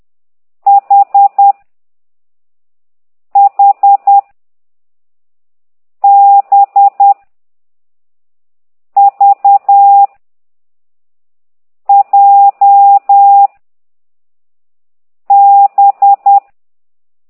Morse code, heard as a keyed tone, about 10 wpm, 800 Hz. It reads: H H B V J B
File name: exerc7